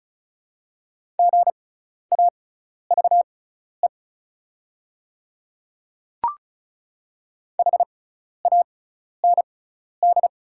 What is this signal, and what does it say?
Morse code, heard as a keyed tone, about 34 wpm, spaced as for 12 wpm, 690 Hz. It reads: GAVE HAND
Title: 6x Character Spacing - Top 500 Words - 35wpm